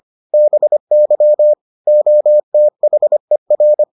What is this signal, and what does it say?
Morse code, heard as a keyed tone, about 25 wpm, 600 Hz. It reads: BY OTHER